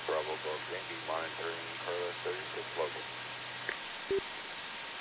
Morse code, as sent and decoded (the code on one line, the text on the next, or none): none